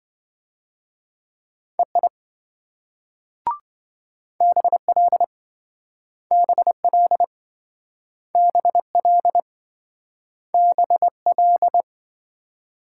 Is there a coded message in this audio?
Yes: Morse code from 1.8 to 11.8 s, heard as a keyed tone, about 27 wpm, 700 Hz. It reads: EI BL BL BL BL